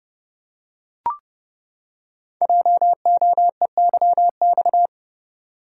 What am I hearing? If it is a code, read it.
Morse code, heard as a keyed tone, about 30 wpm, 700 Hz. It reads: JOEYX
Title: Code Groups - 5 Letter:Numbers - 30wpm